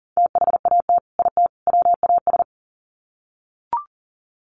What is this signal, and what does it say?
Morse code, heard as a keyed tone, about 40 wpm, 700 Hz. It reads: THAT IT WAS
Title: Sets of 3 Words 40wpm